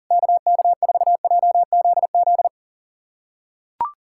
Morse code, heard as a keyed tone, about 40 wpm, 700 Hz. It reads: KK4JZZ
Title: USA Callsign DE Callsign 40wpm